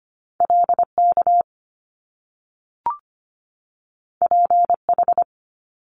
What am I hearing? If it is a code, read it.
Morse code, heard as a keyed tone, about 25 wpm, 700 Hz. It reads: LK PH
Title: Top Two Letter Combinations in Words 25wpm